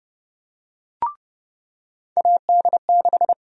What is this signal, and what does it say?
Morse code, heard as a keyed tone, about 31 wpm, 700 Hz. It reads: AD6
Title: USA Call Sign Prefixes - 30wpm